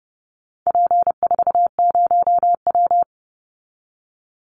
Morse code, heard as a keyed tone, about 30 wpm, 700 Hz. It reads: P40W
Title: Call Signs Repeated 3x - 30wpm